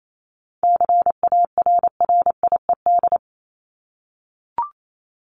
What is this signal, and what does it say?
Morse code, heard as a keyed tone, about 28 wpm, 700 Hz. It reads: CARRIED
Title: ED Suffix - Top 100 Words with Base Words 28wpm